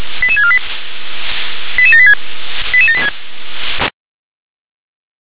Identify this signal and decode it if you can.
Five-tone selective call (ZVEI), heard as quick runs of five tones, tones of about 70 ms. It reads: 88648 88786 88786
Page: /357/SDR/ZVEI/